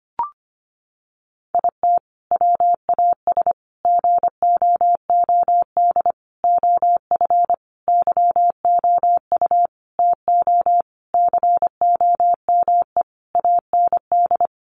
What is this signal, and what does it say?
Morse code, heard as a keyed tone, about 25 wpm, 700 Hz. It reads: IT WAS GOOD OF YOU TO COME AND
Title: Sentences from Top 1100 Words 25wpm